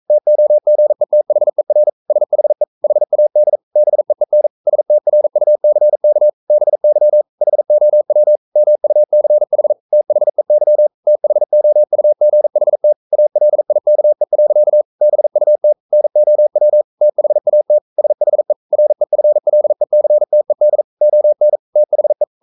Morse code, heard as a keyed tone, about 42 wpm, 600 Hz. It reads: TOGETHER SHE HAD BEEN STRUCK BY HOW MUCH THEY THOUGHT ALIKE. BUT NOW THAT SHE REFLECTED ON THE
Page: Romanen Never af Ken Follet på engelsk.